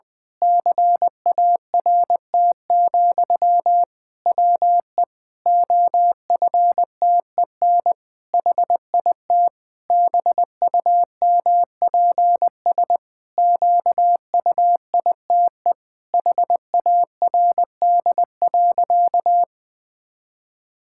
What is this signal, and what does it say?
Morse code, heard as a keyed tone, about 20 wpm, 700 Hz. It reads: CART, WE OFTEN HIT BUMPS QUITE HARD.
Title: Pangram Challenge v2 - 20wpm